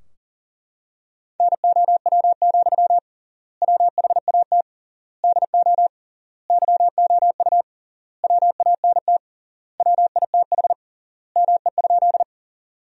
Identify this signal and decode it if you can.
Morse code, heard as a keyed tone, about 40 wpm, 700 Hz. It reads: NOW, WHAT DO YOU WANT WITH ME?
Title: 3x Word-Spacing - Sentences from Top 200 Words - 40wpm